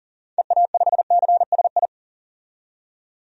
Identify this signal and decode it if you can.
Morse code, heard as a keyed tone, about 40 wpm, 700 Hz. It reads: EA5CSI
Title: DXCC-100 Call Signs - 6 Characters - 40wpm